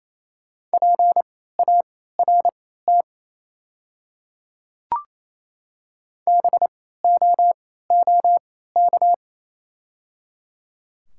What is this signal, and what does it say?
Morse code, heard as a keyed tone, about 28 wpm, spaced as for 16 wpm, 700 Hz. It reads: PART BOOK
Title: 3x Character Spacing - Top 300 Words - 28wpm